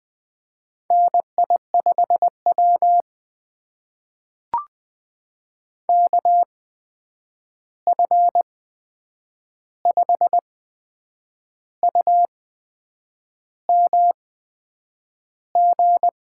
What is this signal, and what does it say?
Morse code, heard as a keyed tone, about 20 wpm, spaced as for 8 wpm, 700 Hz. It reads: NI5W KF5UMG
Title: USA Call Signs - 8x Character Spacing - 20wpm